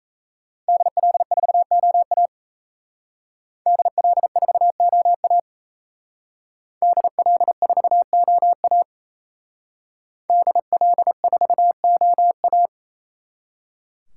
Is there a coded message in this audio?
Yes: Morse code from 0.7 to 12.7 s, heard as a keyed tone, about 36 wpm, 700 Hz. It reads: DL4OA DL4OA DL4OA DL4OA